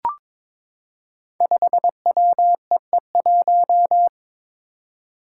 Morse code, heard as a keyed tone, about 22 wpm, 700 Hz. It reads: 5WEE1
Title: Five Challenging Letter-Number Combinations 22wpm